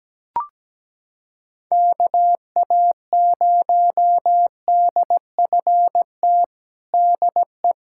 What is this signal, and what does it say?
Morse code, heard as a keyed tone, about 17 wpm, 700 Hz. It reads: KA0DFT DE
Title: USA Callsign DE Callsign - 6 Characters - 17wpm